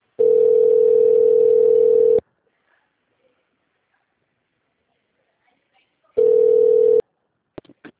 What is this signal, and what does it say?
Ringback tone (North America), 440 + 480 Hz.
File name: out- -2001- - - .1994